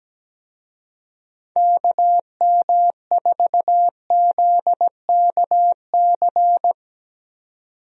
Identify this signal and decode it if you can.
Morse code, heard as a keyed tone, about 17 wpm, 700 Hz. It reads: KM4ZKC
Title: USA Call Signs - 6 Characters - 17wpm